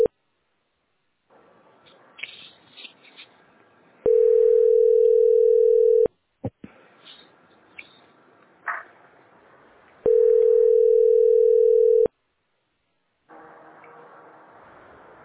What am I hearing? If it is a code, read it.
Ringback tone (North America), 440 + 480 Hz.